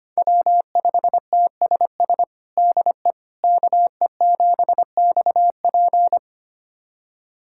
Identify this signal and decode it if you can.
Morse code, heard as a keyed tone, about 25 wpm, 700 Hz. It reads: W5TSS DE KE7XP
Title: USA Callsign DE Callsign 25wpm